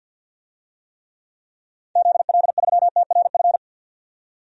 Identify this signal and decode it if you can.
Morse code, heard as a keyed tone, about 50 wpm, 700 Hz. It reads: ZL3TRF
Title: DXCC-100 Call Signs - 6 Characters - 50wpm